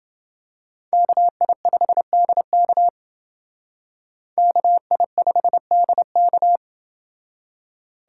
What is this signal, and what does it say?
Morse code, heard as a keyed tone, about 29 wpm, 700 Hz. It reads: KI5DK KI5DK